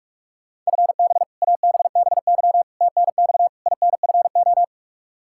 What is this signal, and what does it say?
Morse code, heard as a keyed tone, about 42 wpm, 700 Hz. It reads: FB ABBY TNX INFO
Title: Practice QSO - Exchange 3 - More Info - 45wpm